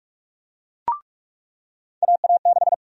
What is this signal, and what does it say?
Morse code, heard as a keyed tone, about 46 wpm, 700 Hz. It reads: AA6
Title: USA Call Sign Prefixes - 45wpm